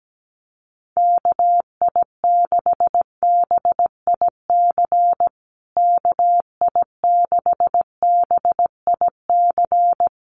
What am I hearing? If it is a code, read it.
Morse code, heard as a keyed tone, about 17 wpm, 700 Hz. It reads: KI6BIC KI6BIC
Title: USA Call Signs Repeated 3x 17wpm